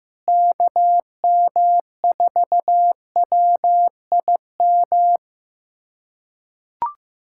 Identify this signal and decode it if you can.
Morse code, heard as a keyed tone, about 15 wpm, 700 Hz. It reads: KM4WIM
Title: USA Callsign DE Callsign - 6 Characters - 15wpm